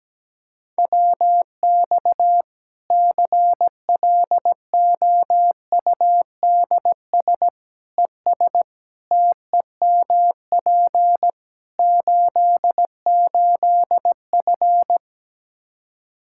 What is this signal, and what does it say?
Morse code, heard as a keyed tone, about 17 wpm, 700 Hz. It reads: WX CLOUDS ES TEMP 88F
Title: K1USN SST QSO Practice 17wpm